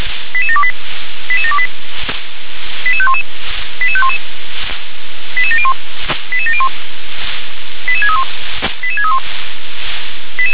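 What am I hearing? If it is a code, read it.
Five-tone selective call (ZVEI), heard as quick runs of five tones, tones of about 70 ms. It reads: 88728 88728 88511 88511 88701 88701 88631 88631
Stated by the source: ZVEI five-tone call received on SDR.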